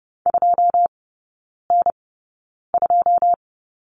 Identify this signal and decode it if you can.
Morse code, heard as a keyed tone, about 30 wpm, spaced as for 9 wpm, 690 Hz. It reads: 2N2
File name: Lesson 16e - Letter c - Three Characters 30wpm